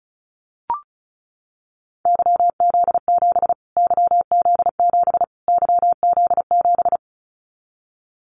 Morse code, heard as a keyed tone, about 34 wpm, 700 Hz. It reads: YZ7 YZ7 YZ7
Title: Callsign Prefixes for Top 25 DXCC - Repeated 3x - 35wpm